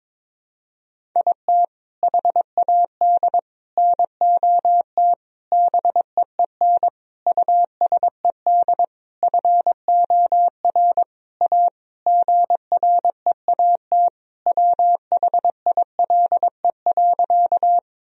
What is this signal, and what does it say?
Morse code, heard as a keyed tone, about 22 wpm, 700 Hz. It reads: IT HAD NOT BEEN USED FOR A GREAT WHILE.
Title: Sentences from Top 1000 Words - Review Part 1 - 22wpm